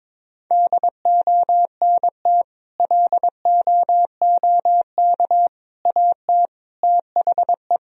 Morse code, heard as a keyed tone, about 21 wpm, 700 Hz. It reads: DONT LOOK AT THE
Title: Sentences from Top 1000 Words - Review Part 1 - 22wpm